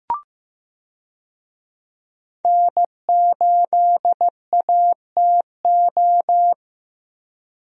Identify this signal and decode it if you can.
Morse code, heard as a keyed tone, about 15 wpm, 700 Hz. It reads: N8ATO
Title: USA Call Signs - 5 Characters - Repeated 3x - 15wpm